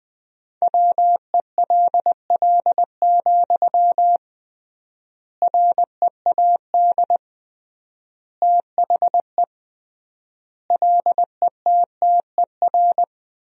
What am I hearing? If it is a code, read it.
Morse code, heard as a keyed tone, about 20 wpm, 700 Hz. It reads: WELL, READ THE LETTER
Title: 3x Word-Spacing - Sentences from Top 300 Words - 20wpm